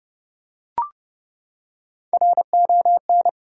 Morse code, heard as a keyed tone, about 30 wpm, 700 Hz. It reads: RON
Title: USA Common Names - Max 4 Letters - 30wpm